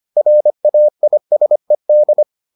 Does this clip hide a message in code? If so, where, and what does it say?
Yes: Morse code from 0.2 to 2.2 s, heard as a keyed tone, about 25 wpm, 600 Hz. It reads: RAISED